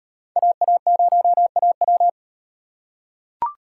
Morse code, heard as a keyed tone, about 34 wpm, 700 Hz. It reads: AA0AW